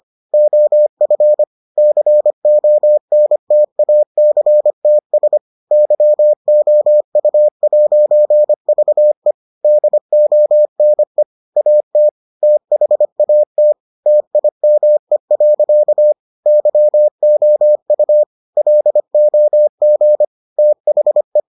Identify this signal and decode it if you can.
Morse code, heard as a keyed tone, about 25 wpm, 600 Hz. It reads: OF CONTACTS YOU'VE DONE AT THAT TIME. YOU LOG THE